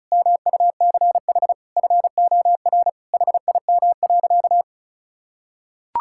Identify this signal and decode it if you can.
Morse code, heard as a keyed tone, about 36 wpm, 700 Hz. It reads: MUCH FOR HIM.
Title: Sentences from Top 1000 Words - Review Part 5 - 35wpm